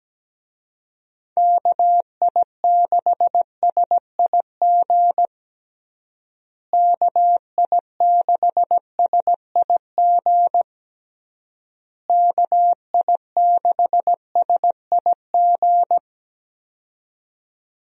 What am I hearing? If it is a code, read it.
Morse code, heard as a keyed tone, about 17 wpm, 700 Hz. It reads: KI6SIG KI6SIG KI6SIG
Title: USA Call Signs - 6 Characters - Repeated 3x - 17wpm